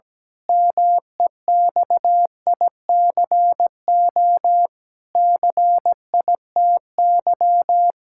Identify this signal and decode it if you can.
Morse code, heard as a keyed tone, about 17 wpm, 700 Hz. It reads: MEXICO CITY
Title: Pangram Challenge v2 - 17wpm